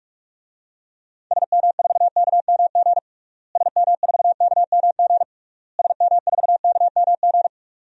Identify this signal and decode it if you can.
Morse code, heard as a keyed tone, about 46 wpm, 700 Hz. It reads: SM4KMG SM4KMG SM4KMG
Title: DXCC-100 Call Signs - 6 Characters - Repeated 3x - 45wpm